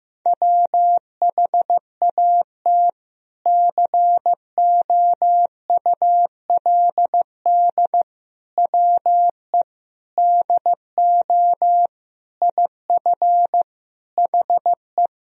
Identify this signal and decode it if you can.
Morse code, heard as a keyed tone, about 15 wpm, 700 Hz. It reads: WHAT COULD WE DO IF HE
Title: Sentences from Top 400 Words and Abbreviations 15wpm